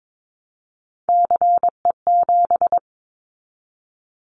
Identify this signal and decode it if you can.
Morse code, heard as a keyed tone, about 22 wpm, 700 Hz. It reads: CE7
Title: Callsign Prefixes for Top 100 DXCC - 22wpm